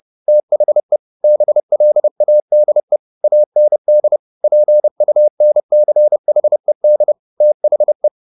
Morse code, heard as a keyed tone, about 30 wpm, 600 Hz. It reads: THE BLADE AND PUNCHED THE